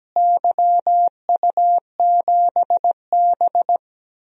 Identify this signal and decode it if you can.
Morse code, heard as a keyed tone, about 17 wpm, 700 Hz. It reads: YU7B
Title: DXCC-100 Call Signs - 4 Characters - Repeated 3x - 17wpm